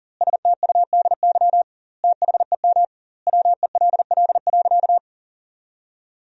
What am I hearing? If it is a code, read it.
Morse code, heard as a keyed tone, about 40 wpm, 700 Hz. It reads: STUDY THEM WELL.
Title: 2x Word-Spacing - Sentences from Top 300 Words - 40wpm